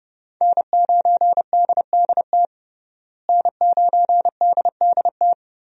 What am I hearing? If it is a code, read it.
Morse code, heard as a keyed tone, about 32 wpm, 700 Hz. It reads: N9DDT N9DDT
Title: USA Call Signs - 5 Characters - Repeated 3x - 30wpm